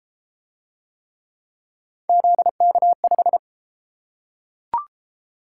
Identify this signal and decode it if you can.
Morse code, heard as a keyed tone, about 32 wpm, 700 Hz. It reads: ZK5